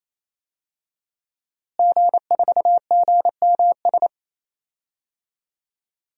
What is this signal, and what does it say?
Morse code, heard as a keyed tone, about 28 wpm, 700 Hz. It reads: G4GMS
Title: DXCC-100 Call Signs - 5 Characters - Repeated 3x - 28wpm